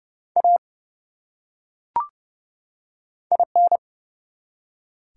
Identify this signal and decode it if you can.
Morse code, heard as a keyed tone, about 30 wpm, 700 Hz. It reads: A IN
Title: Top 180 QSO Elements 30wpm